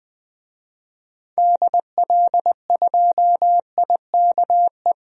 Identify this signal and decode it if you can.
Morse code, heard as a keyed tone, about 20 wpm, 700 Hz. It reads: DL2IKE
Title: DXCC-100 Call Signs - 6 Characters - 20wpm